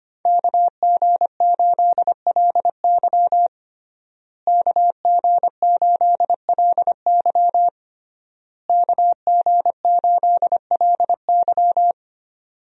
Morse code, heard as a keyed tone, about 25 wpm, 700 Hz. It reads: KG8LY KG8LY KG8LY
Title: USA Call Signs - 5 Characters - Repeated 3x - 25wpm